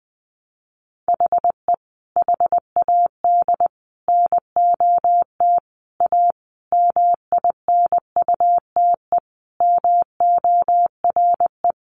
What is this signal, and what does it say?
Morse code, heard as a keyed tone, about 20 wpm, 700 Hz. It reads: HE HAD NOT A MINUTE MORE
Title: Sentences from Top 1100 Words 20wpm